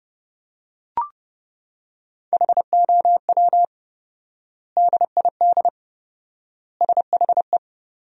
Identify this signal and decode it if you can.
Morse code, heard as a keyed tone, about 32 wpm, 700 Hz. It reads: HOW DID SHE
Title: 4x Word-Spacing - Sentences from Top 100 Words - 30wpm